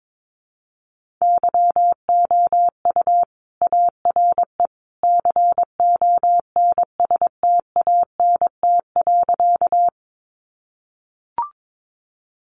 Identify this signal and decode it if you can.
Morse code, heard as a keyed tone, about 22 wpm, 700 Hz. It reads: YOU ARE CONSTANT.
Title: Sentences from Top 1500 Words 22wpm